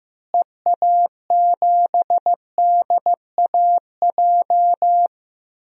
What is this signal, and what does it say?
Morse code, heard as a keyed tone, about 15 wpm, 700 Hz. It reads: EA7DAJ
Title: DXCC-100 Call Signs - 6 Characters - Repeated 2x - 15wpm